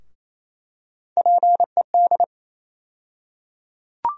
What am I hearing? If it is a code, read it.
Morse code, heard as a keyed tone, about 28 wpm, 700 Hz. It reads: PED
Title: USA Call Sign Postfixes - 3 Letters 28wpm